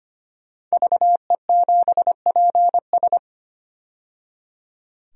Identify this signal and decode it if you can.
Morse code, heard as a keyed tone, about 25 wpm, 700 Hz. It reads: VE7PS
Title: DXCC-100 Call Signs - 5 Characters - 25wpm